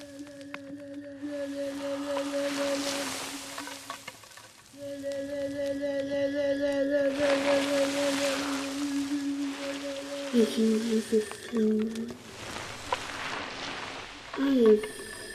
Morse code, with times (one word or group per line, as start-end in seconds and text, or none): none